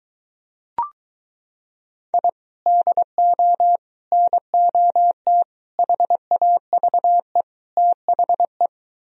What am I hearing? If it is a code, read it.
Morse code, heard as a keyed tone, about 22 wpm, 700 Hz. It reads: I DO NOT HAVE THE